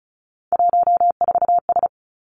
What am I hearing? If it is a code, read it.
Morse code, heard as a keyed tone, about 36 wpm, 700 Hz. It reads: 14S
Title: Three Challenging Letter-Number Combinations 35wpm